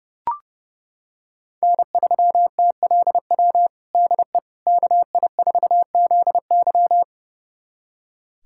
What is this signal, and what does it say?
Morse code, heard as a keyed tone, about 32 wpm, 700 Hz. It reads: N3TLW DE KI4ZY
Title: USA Callsign DE Callsign - 5 Characters - 30wpm